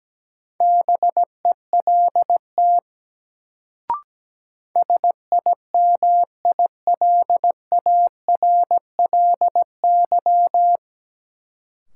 Morse code, heard as a keyed tone, about 17 wpm, 700 Hz. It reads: BELT SIMILARLY
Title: Sentences from Top 1400 Words - New Word Drill Alt - 17wpm